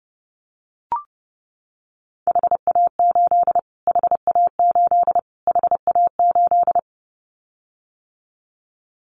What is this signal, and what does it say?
Morse code, heard as a keyed tone, about 30 wpm, 700 Hz. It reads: HA8 HA8 HA8
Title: Callsign Prefixes for Top 25 DXCC - Repeated 3x - 30wpm